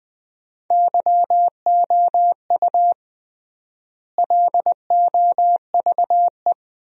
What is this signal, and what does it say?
Morse code, heard as a keyed tone, about 20 wpm, 700 Hz. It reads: YOU LOVE
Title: 3x Word-Spacing - Sentences from Top 500 Words - 20wpm